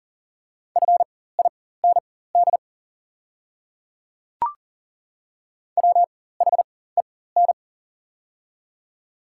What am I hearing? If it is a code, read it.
Morse code, heard as a keyed tone, about 40 wpm, spaced as for 19 wpm, 690 Hz. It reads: FIND WHEN
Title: 4x Character Spacing - Top 100 Words - 40wpm